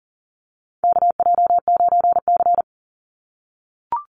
Morse code, heard as a keyed tone, about 40 wpm, 700 Hz. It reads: KJ9C
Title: Call Signs - 40wpm